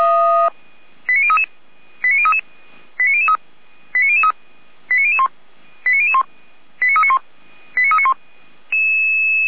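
Five-tone selective call (ZVEI), heard as quick runs of five tones, tones of about 70 ms. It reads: 89033 89033 89003 89003 89001 89001 89391 89391
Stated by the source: ZVEI five-tone call received by SDR.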